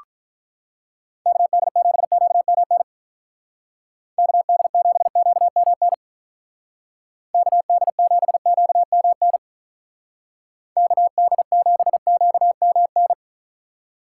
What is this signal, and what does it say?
Morse code, heard as a keyed tone, about 45 wpm, 700 Hz. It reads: KD7QMN KD7QMN KD7QMN KD7QMN